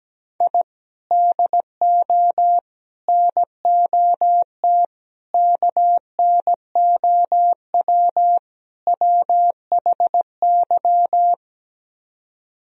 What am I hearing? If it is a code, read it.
Morse code, heard as a keyed tone, about 17 wpm, 700 Hz. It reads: I DO NOT KNOW WHY
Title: Speed-Racing - Sets of 5 Words 17wpm